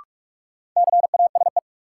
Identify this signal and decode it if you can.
Morse code, heard as a keyed tone, about 47 wpm, 700 Hz. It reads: CASE
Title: Sentences from Top 700 Words - New Word Drill - 45wpm